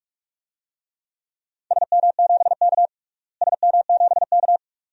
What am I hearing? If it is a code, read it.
Morse code, heard as a keyed tone, about 41 wpm, 700 Hz. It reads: SM7K SM7K
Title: DXCC-100 Call Signs - 4 Characters - Repeated 2x - 45wpm